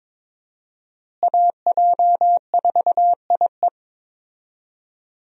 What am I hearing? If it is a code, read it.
Morse code, heard as a keyed tone, about 22 wpm, 700 Hz. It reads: AJ4IE
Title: USA Call Signs - 5 Characters - Repeated 3x - 22wpm